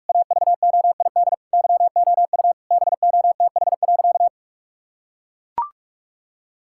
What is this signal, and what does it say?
Morse code, heard as a keyed tone, about 41 wpm, 700 Hz. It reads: AVOID YOU BOTH.
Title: Sentences from Top 1000 Words - Review Part 6 - 45wpm